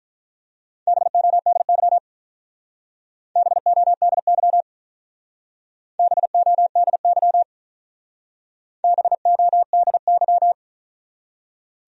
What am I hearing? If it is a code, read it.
Morse code, heard as a keyed tone, about 43 wpm, 700 Hz. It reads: BODY BODY BODY BODY
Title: Speed-Racing Top 500 Words 35wpm